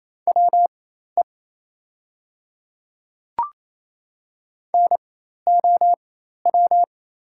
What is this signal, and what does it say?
Morse code, heard as a keyed tone, about 28 wpm, spaced as for 13 wpm, 700 Hz. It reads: WE NOW
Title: 4x Character Spacing - Top 100 Words - 28wpm